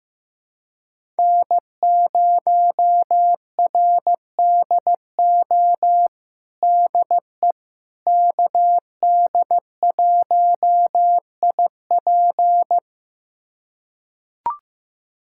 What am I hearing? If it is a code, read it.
Morse code, heard as a keyed tone, about 15 wpm, 700 Hz. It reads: N0RDO DE KD1IP
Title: USA Callsign DE Callsign - 5 Characters - 15wpm